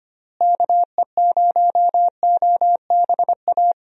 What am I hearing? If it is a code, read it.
Morse code, heard as a keyed tone, about 25 wpm, 700 Hz. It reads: KE0OBA
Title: USA Call Signs - 6 Characters - Repeated 2x - 25wpm